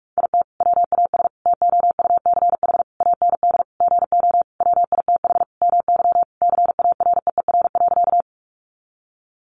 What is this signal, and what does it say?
Morse code, heard as a keyed tone, about 44 wpm, 700 Hz. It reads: IT WAS TOUCH AND GO WITH MY CAREER.
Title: Sentences from Top 1200 Words 45wpm